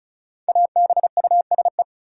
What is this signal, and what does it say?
Morse code, heard as a keyed tone, about 36 wpm, 700 Hz. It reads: ABUSE
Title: Sentences from Top 1300 Words - New Word Drill - 35wpm